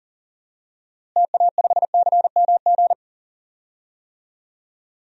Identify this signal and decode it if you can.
Morse code, heard as a keyed tone, about 40 wpm, 700 Hz. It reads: TA5CMG
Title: DXCC-100 Call Signs - 6 Characters - Repeated 2x - 40wpm